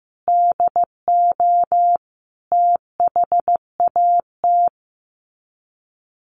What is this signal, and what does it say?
Morse code, heard as a keyed tone, about 15 wpm, 700 Hz. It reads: DO THAT
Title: Sets of 5 Words 15wpm